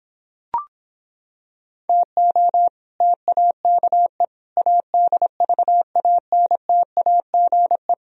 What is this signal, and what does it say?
Morse code, heard as a keyed tone, about 26 wpm, 700 Hz. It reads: TO TAKE ADVANTAGE